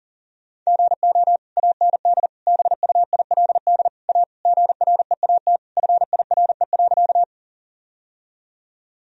Morse code, heard as a keyed tone, about 40 wpm, 700 Hz. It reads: GO AND BUILD A GREAT FIRE.
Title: Sentences from Top 1000 Words - Review Part 1 - 40wpm